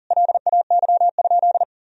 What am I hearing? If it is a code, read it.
Morse code, heard as a keyed tone, about 40 wpm, 700 Hz. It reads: LAY?
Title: Sentences from Top 400 Words and Abbreviations 40wpm